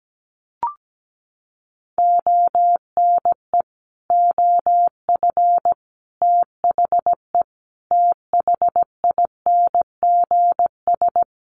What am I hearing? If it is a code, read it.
Morse code, heard as a keyed tone, about 17 wpm, 700 Hz. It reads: ONE OF THE THINGS
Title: Sets of 5 Words 17wpm